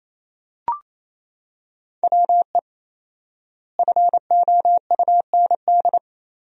Morse code, heard as a keyed tone, about 28 wpm, 700 Hz. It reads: WE FOUND
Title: 4x Word-Spacing - Sentences from Top 300 Words - 28wpm